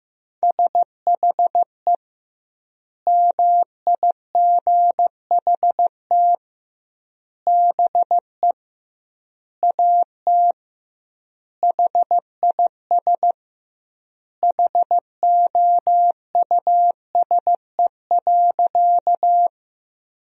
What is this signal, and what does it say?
Morse code, heard as a keyed tone, about 15 wpm, 700 Hz. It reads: SHE MIGHT BE AT HIS HOUSE.
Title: 2x Word-Spacing - Sentences from Top 300 Words - 15wpm